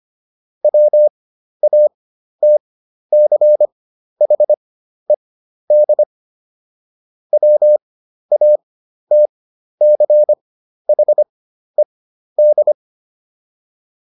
Morse code, heard as a keyed tone, about 25 wpm, spaced as for 12 wpm, 600 Hz. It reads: WATCHED WATCHED